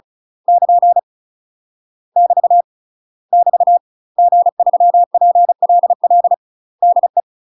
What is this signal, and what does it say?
Morse code, heard as a keyed tone, about 36 wpm, 700 Hz. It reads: <KN> <BT> <BT> G3PLL DE
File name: UK MASTER QSOs TRAINING / 35WPM / UK MASTER QSOs 0014